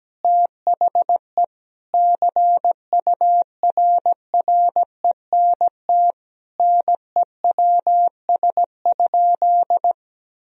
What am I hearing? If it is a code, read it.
Morse code, heard as a keyed tone, about 17 wpm, 700 Hz. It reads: THE CURRENT NEWS?
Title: Sentences from Top 1000 Words - Review Part 4 - 17wpm